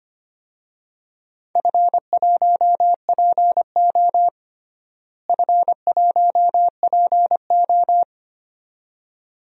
Morse code, heard as a keyed tone, about 25 wpm, 700 Hz. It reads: F1PO F1PO
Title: DXCC-100 Call Signs - 4 Characters - Repeated 2x - 25wpm